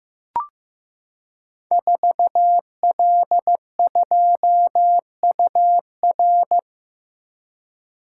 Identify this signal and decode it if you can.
Morse code, heard as a keyed tone, about 15 wpm, 700 Hz. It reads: 4L2UR
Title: Code Groups - 5 Letter:Numbers - 15wpm